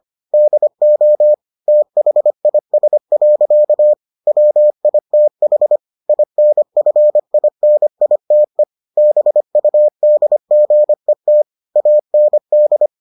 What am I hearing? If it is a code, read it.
Morse code, heard as a keyed tone, about 25 wpm, 600 Hz. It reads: DO THIS. WITH INFINITE BUDGET AND